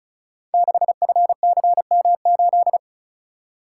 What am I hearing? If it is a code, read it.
Morse code, heard as a keyed tone, about 36 wpm, 700 Hz. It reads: 6FCM8
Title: Code Groups - 5 Letter:Numbers - 35wpm